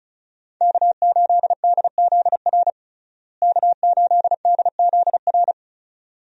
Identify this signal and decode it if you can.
Morse code, heard as a keyed tone, about 34 wpm, 700 Hz. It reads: K8DZR K8DZR
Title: USA Call Signs - 5 Characters - Repeated 2x - 35wpm